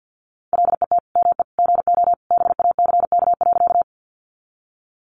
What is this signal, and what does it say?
Morse code, heard as a keyed tone, about 50 wpm, 700 Hz. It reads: LET ME GO BACK.
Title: Sentences from Top 300 Words 50wpm